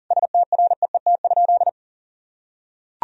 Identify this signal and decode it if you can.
Morse code, heard as a keyed tone, about 37 wpm, 700 Hz. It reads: STREET?
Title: Sentences from Top 1000 Words - Review Part 2 - 40wpm